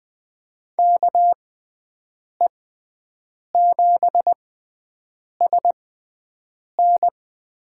Morse code, heard as a keyed tone, about 20 wpm, spaced as for 7 wpm, 700 Hz. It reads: KE7SN
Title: USA Call Signs - 6 Characters - 6x Character Spacing - 20wpm